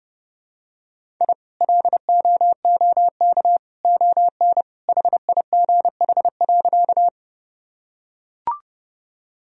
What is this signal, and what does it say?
Morse code, heard as a keyed tone, about 32 wpm, 700 Hz. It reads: I LOOK ON HIGH.
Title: Sentences from Top 1000 Words - Review Part 1 - 30wpm